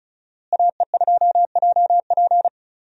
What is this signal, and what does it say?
Morse code, heard as a keyed tone, about 36 wpm, 700 Hz. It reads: AE2JP
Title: USA Call Signs - 5 Characters - Repeated 3x - 35wpm